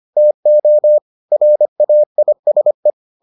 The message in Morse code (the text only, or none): TO RAISE